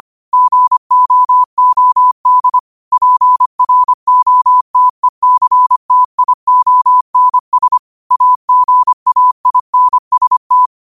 Morse code, heard as a keyed tone, about 25 wpm, 1000 Hz. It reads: GOOD PROTECTIONS AGAINST